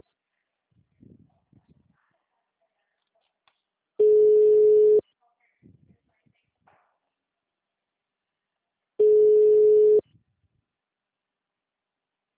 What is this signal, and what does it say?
Ringback tone (Europe, 425 Hz), with speech.
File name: out- -2023